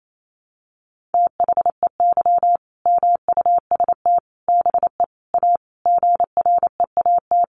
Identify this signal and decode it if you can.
Morse code, heard as a keyed tone, about 28 wpm, 700 Hz. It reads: THEY MUST BE A GREAT
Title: Sentences from Top 1400 Words 28wpm